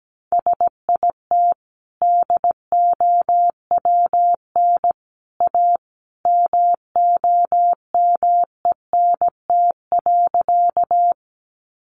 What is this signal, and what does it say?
Morse code, heard as a keyed tone, about 17 wpm, 700 Hz. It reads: SIT DOWN A MOMENT.
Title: Sentences from Top 1000 Words 17wpm